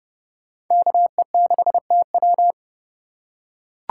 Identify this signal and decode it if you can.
Morse code, heard as a keyed tone, about 28 wpm, 700 Hz. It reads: KE6TW